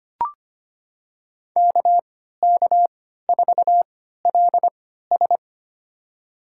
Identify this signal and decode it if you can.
Morse code, heard as a keyed tone, about 25 wpm, spaced as for 14 wpm, 700 Hz. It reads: KK4LS
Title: USA Call Signs - 5 Characters - 3x Character Spacing - 25wpm